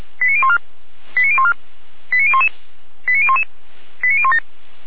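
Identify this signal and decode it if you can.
Five-tone selective call (ZVEI), heard as quick runs of five tones, tones of about 70 ms. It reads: 89014 89014 89011 89011 89017
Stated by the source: ZVEI five-tone selective call received on SDR.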